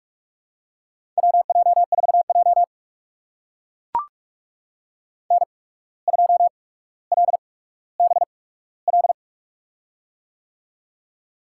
Morse code, heard as a keyed tone, about 44 wpm, spaced as for 18 wpm, 700 Hz. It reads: WJ4J N2LBL
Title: USA Call Signs - 8x Character Spacing - 45wpm